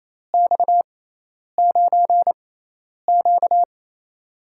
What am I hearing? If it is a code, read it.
Morse code, heard as a keyed tone, about 28 wpm, spaced as for 10 wpm, 700 Hz. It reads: X9Q
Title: USA Call Signs - 6x Character Spacing - 28wpm